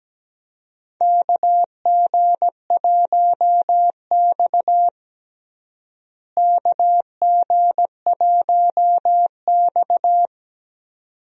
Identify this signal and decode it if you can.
Morse code, heard as a keyed tone, about 17 wpm, 700 Hz. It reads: KG1X KG1X
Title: USA Call Signs - 4 Characters - Repeated 2x - 17wpm